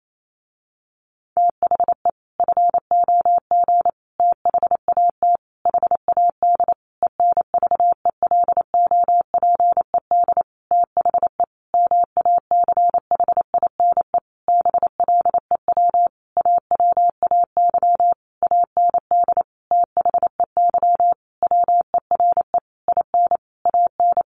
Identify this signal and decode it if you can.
Morse code, heard as a keyed tone, about 28 wpm, 700 Hz. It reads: THE FOG THAT HAD ENVELOPED THE MACHINE BLEW AWAY AND THEY WERE IN AN
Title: A Sound Of Thunder 28wpm